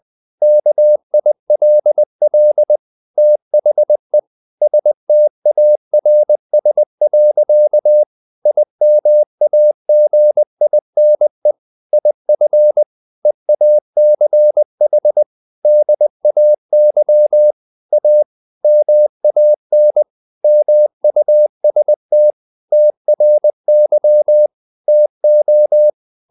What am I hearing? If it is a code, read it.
Morse code, heard as a keyed tone, about 20 wpm, 600 Hz. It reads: KILL THE STARS. IMAGINE IF EACH DAY A MAN MUST TRY TO
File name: the old man and the sea part 2 20wpm